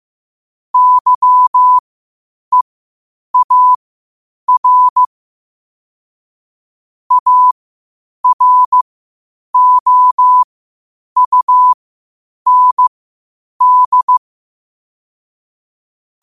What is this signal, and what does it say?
Morse code, heard as a keyed tone, about 15 wpm, spaced as for 8 wpm, 1000 Hz. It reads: YEAR AROUND
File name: qotdfiles / 8wpm